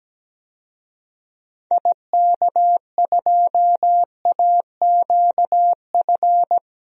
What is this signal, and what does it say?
Morse code, heard as a keyed tone, about 17 wpm, 700 Hz. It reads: IK2AQF
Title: DXCC-100 Call Signs - 6 Characters - 17wpm